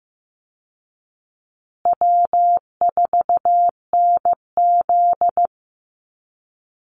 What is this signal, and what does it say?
Morse code, heard as a keyed tone, about 15 wpm, 700 Hz. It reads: W4NZ
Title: Call Signs Repeated 2x - 15wpm